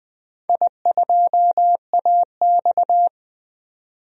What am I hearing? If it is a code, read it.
Morse code, heard as a keyed tone, about 20 wpm, 700 Hz. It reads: I2AX
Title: DXCC-100 Call Signs - 4 Characters - 20wpm